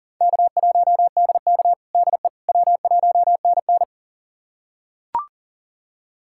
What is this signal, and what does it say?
Morse code, heard as a keyed tone, about 37 wpm, 700 Hz. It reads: K1DK DE W1NN
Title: USA Callsign DE Callsign - 4 Characters - 40wpm